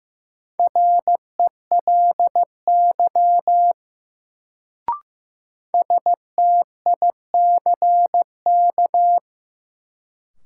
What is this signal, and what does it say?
Morse code, heard as a keyed tone, about 15 wpm, 700 Hz. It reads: RELY STICK
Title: Sentences from Top 1200 Words - New Word Drill Alt - 15wpm